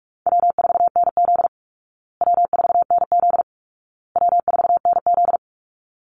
Morse code, heard as a keyed tone, about 44 wpm, 700 Hz. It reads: W4NZ W4NZ W4NZ
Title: Easy Call Signs Repeated 3x - 45wpm